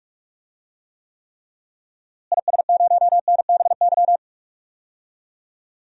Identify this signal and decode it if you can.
Morse code, heard as a keyed tone, about 44 wpm, 700 Hz. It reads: IS0NBY
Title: DXCC-100 Call Signs - 6 Characters - Repeated 3x - 45wpm